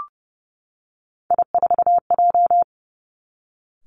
Morse code, heard as a keyed tone, about 30 wpm, 700 Hz. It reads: I4J
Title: Three Challenging Letter-Number Combinations 30wpm